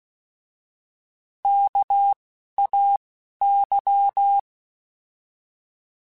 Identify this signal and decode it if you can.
Morse code, heard as a keyed tone, about 16 wpm, 800 Hz. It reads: K A Y
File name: Alphabetical / Alphabetical mp3s / Ks @10